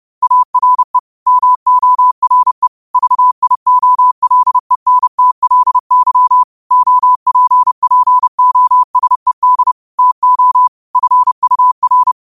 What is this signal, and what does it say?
Morse code, heard as a keyed tone, about 30 wpm, 1000 Hz. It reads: ARE MORE VIOLENTLY OPPOSED TO FUR
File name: qotdfiles / 30wpm